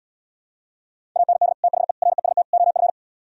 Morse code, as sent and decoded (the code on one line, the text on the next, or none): .- -. -.. / - .... / .-. ... - / --- ..-.
AND THE REST OF